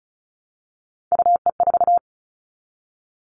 Morse code, heard as a keyed tone, about 36 wpm, 700 Hz. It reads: UE4
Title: Three Challenging Letter-Number Combinations 35wpm